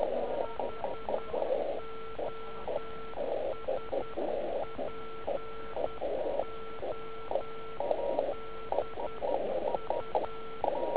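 Morse code, heard as a keyed tone, about 10 wpm, 1340 Hz. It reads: HOSPODU
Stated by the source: Slow (QRS) CW test by beginner OK1IFW during beginner Morse skeds on 3559.5 kHz (80 m).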